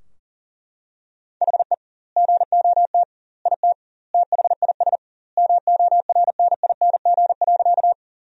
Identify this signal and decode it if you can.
Morse code, heard as a keyed tone, about 37 wpm, 700 Hz. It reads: HE GOT IT THIS MORNING.
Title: 2x Word-Spacing - Sentences from Top 400 Words - 40wpm